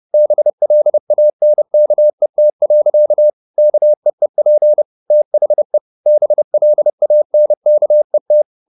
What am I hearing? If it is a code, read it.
Morse code, heard as a keyed tone, about 30 wpm, 600 Hz. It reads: BLANKET. KEEP THE BLANKET
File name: the old man and the sea / the old man and the sea 1 30wpm